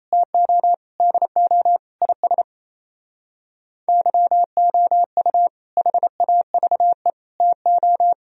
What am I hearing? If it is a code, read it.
Morse code, heard as a keyed tone, about 37 wpm, 700 Hz. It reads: TO DO IS YOU HAVE TO